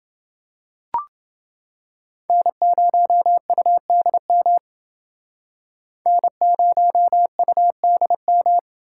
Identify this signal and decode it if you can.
Morse code, heard as a keyed tone, about 30 wpm, 700 Hz. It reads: N0UDM N0UDM